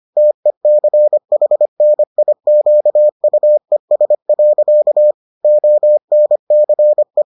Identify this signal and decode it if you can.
Morse code, heard as a keyed tone, about 25 wpm, 600 Hz. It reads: TECHNIQUES. ONCE